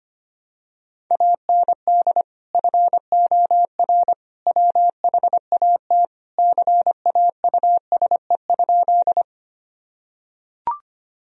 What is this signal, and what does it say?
Morse code, heard as a keyed tone, about 25 wpm, 700 Hz. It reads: AND FOR WHAT CAUSE?
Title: Sentences from Top 1000 Words - Review Part 1 - 25wpm